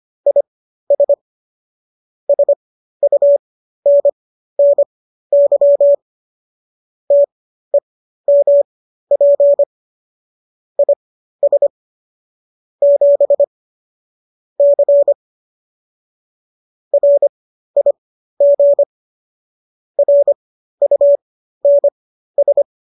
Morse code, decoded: IS SUNNY TEMP IS 7 C RIG RUNS

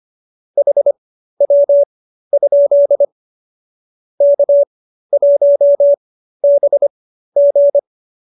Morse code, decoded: HW? K1BG

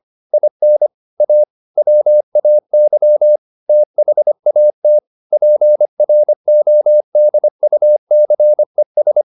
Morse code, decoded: IN A WAY THAT PRODUCES

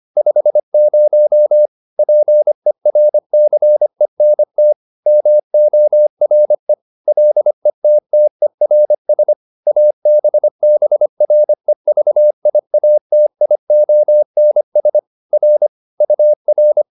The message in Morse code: 50 PERCENT MORE LETTERS ABBREVIATIONS R UR

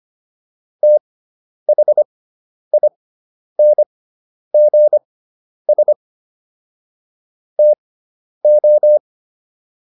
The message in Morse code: THINGS TO

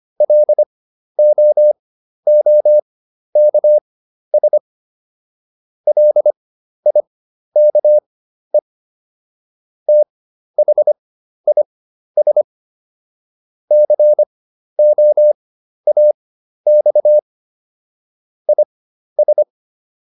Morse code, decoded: LOOKS LIKE THIS COAX IS

